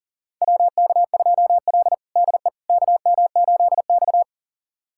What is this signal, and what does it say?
Morse code, heard as a keyed tone, about 37 wpm, 700 Hz. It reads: WK2L DE KM8X
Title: USA Callsign DE Callsign - 4 Characters - 40wpm